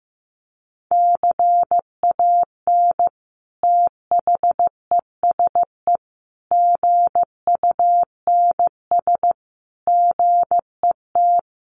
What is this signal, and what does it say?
Morse code, heard as a keyed tone, about 14 wpm, 700 Hz. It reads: CAN THESE GUNS GET
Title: A Sound Of Thunder 15wpm